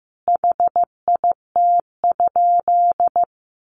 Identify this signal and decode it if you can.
Morse code, heard as a keyed tone, about 15 wpm, 700 Hz. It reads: HIT?
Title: Sentences from Top 1200 Words 15wpm